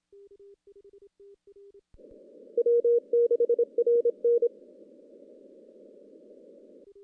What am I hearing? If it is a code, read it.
Morse code, heard as a keyed tone, about 26 wpm, 470 Hz. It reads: W6RN
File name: CW / CW-SS-k5tr- -1700